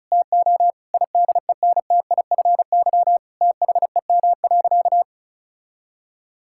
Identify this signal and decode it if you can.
Morse code, heard as a keyed tone, about 36 wpm, 700 Hz. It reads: TO IDENTIFY THEM.
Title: Sentences from Top 1000 Words - Review Part 6 - 35wpm